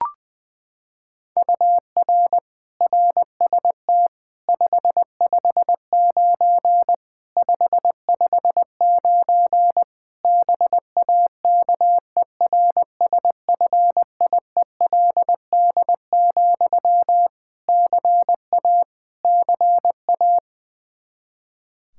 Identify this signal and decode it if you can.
Morse code, heard as a keyed tone, about 20 wpm, 700 Hz. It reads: UR RST 559 559 BAKERSFIELD, CA CA
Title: K1USN SST QSO Practice 20wpm